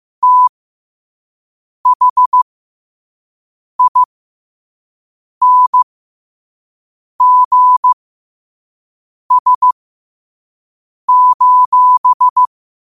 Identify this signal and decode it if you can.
Morse code, heard as a keyed tone, about 15 wpm, spaced as for 5 wpm, 1000 Hz. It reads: THINGS:
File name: qotdfiles / 5wpm